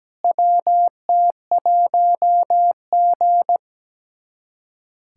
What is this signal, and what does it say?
Morse code, heard as a keyed tone, about 17 wpm, 700 Hz. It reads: WT1G
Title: USA Callsign DE Callsign - 4 Characters - 17wpm